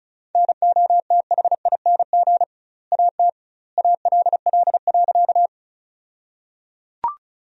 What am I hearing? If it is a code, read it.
Morse code, heard as a keyed tone, about 34 wpm, 700 Hz. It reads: NOTHING AT ALL.
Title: 2x Word-Spacing - Sentences from Top 500 Words - 35wpm